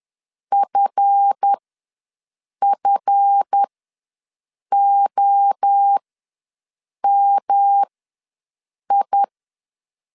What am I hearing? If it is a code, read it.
Morse code, heard as a keyed tone, about 10 wpm, spaced as for 6 wpm, 800 Hz. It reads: FFOMI
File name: LEZIONI / 7 GF9Z / 7 GF ws 10wpm 3918